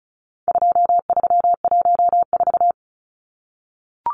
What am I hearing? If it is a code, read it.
Morse code, heard as a keyed tone, about 34 wpm, 700 Hz. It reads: 2314
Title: Numbers - 4-Digit 35wpm